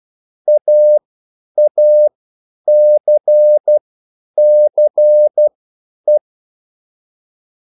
Morse code, heard as a keyed tone, about 12 wpm, spaced as for 9 wpm, 600 Hz. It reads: AACCE
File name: QSL / CW lesson